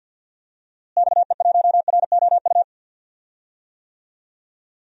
Morse code, heard as a keyed tone, about 52 wpm, 700 Hz. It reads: XE1ROU
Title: DXCC-100 Call Signs - 6 Characters - Repeated 3x - 50wpm